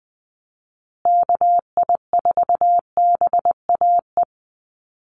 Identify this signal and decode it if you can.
Morse code, heard as a keyed tone, about 20 wpm, 700 Hz. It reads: KI4BAE
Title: USA Call Signs Repeated 2x 20wpm